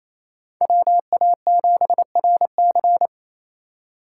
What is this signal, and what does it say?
Morse code, heard as a keyed tone, about 28 wpm, 700 Hz. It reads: WA7RC
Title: USA Call Signs - 5 Characters - 4x Character Spacing - 28wpm